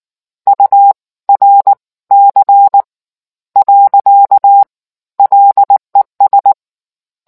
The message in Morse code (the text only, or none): URC . LES